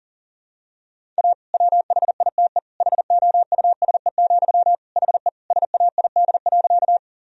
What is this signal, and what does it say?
Morse code, heard as a keyed tone, about 40 wpm, 700 Hz. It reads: A WHITE HOUSE, HE SAID.
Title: Sentences from Top 1000 Words - Review Part 1 - 40wpm